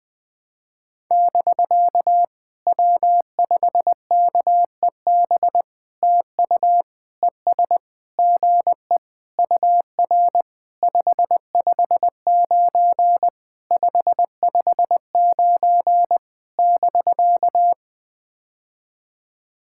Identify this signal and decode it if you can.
Morse code, heard as a keyed tone, about 20 wpm, 700 Hz. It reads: <BK> W5KEB TU ES GE UR 559 559 <BK>
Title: POTA QSO Practice - 20wpm